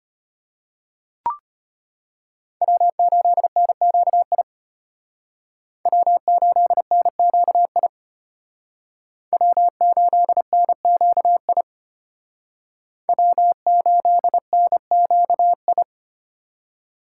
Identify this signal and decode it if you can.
Morse code, heard as a keyed tone, about 33 wpm, 700 Hz. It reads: W8NQI W8NQI W8NQI W8NQI